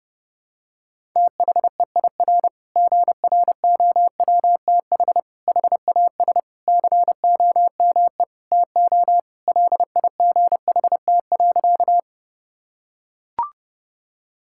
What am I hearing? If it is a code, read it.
Morse code, heard as a keyed tone, about 32 wpm, 700 Hz. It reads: THEIR GROWTH HAS COME TO LIGHT.
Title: Sentences from Top 1000 Words - Review Part 5 - 30wpm